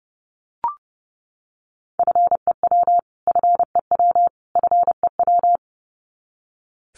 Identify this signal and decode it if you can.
Morse code, heard as a keyed tone, about 30 wpm, 700 Hz. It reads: FEW FEW FEW
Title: Top 500 Words Repeated 30wpm